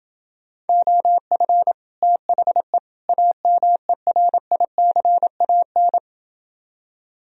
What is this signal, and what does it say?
Morse code, heard as a keyed tone, about 30 wpm, 700 Hz. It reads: OF THE AMERICAN